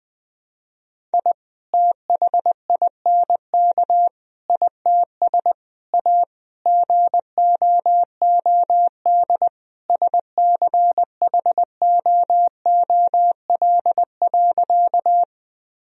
Morse code, heard as a keyed tone, about 20 wpm, 700 Hz. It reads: I THINK ITS A GOOD SCHOOL.
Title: Sentences from Top 1000 Words - Review Part 1 - 20wpm